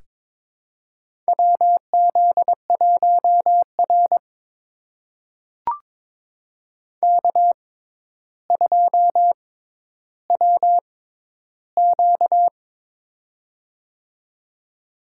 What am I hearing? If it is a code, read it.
Morse code, heard as a keyed tone, about 22 wpm, spaced as for 11 wpm, 700 Hz. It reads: WZ1R K2WQ
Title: USA Call Signs - 4 Characters - 6x Character Spacing - 22wpm